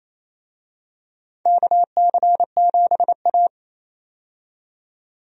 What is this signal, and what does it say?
Morse code, heard as a keyed tone, about 28 wpm, 700 Hz. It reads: KC7A
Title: USA Call Signs - 4 Characters - 28wpm